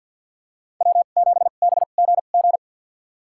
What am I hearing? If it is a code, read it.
Morse code, heard as a keyed tone, about 52 wpm, spaced as for 36 wpm, 700 Hz. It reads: W7BGG
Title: USA Call Signs - 2x Character Spacing - 50wpm